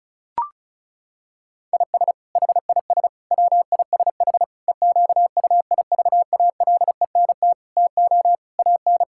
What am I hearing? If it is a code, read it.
Morse code, heard as a keyed tone, about 36 wpm, 700 Hz. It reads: IS HIS WISH EQUIVALENT TO AN
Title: Sentences from Top 1600 Words 35wpm